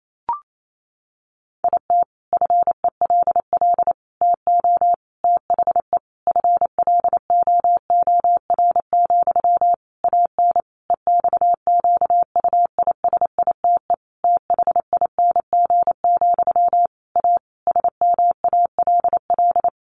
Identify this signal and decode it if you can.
Morse code, heard as a keyed tone, about 28 wpm, 700 Hz. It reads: IT FELL TO THE FLOOR, AN EXQUISITE THING, A SMALL
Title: A Sound Of Thunder 28wpm